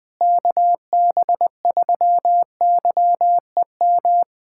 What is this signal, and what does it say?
Morse code, heard as a keyed tone, about 21 wpm, 700 Hz. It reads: KB3YEM
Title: USA Callsign DE Callsign 20wpm